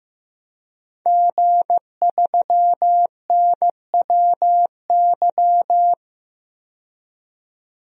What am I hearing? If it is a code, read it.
Morse code, heard as a keyed tone, about 15 wpm, 700 Hz. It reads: G3NWY
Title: DXCC-100 Call Signs - 5 Characters - 15wpm